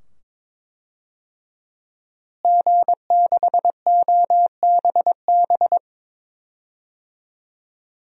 Morse code, heard as a keyed tone, about 21 wpm, 700 Hz. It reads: G6OBB
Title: DXCC-100 Call Signs - 5 Characters - Repeated 2x - 22wpm